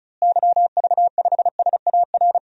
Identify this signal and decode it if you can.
Morse code, heard as a keyed tone, about 36 wpm, 700 Hz. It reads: YV5SAR
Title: DXCC-100 Call Signs - 6 Characters - Repeated 3x - 35wpm